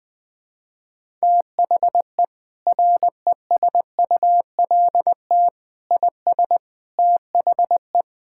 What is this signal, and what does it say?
Morse code, heard as a keyed tone, about 20 wpm, 700 Hz. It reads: THE RESULT IS THE
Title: Sentences from Top 1000 Words - Review Part 5 - 20wpm